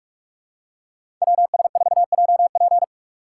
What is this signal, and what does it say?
Morse code, heard as a keyed tone, about 46 wpm, 700 Hz. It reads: WS4JP
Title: USA Call Signs - 5 Characters - 6x Character Spacing - 45wpm